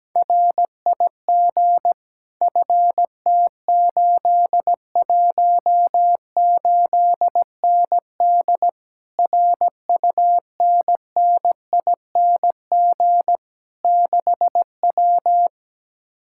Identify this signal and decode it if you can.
Morse code, heard as a keyed tone, about 17 wpm, 700 Hz. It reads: RIG FT818ND RUNNING 6W
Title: Speed-Racing Common QSO Phrases - 17wpm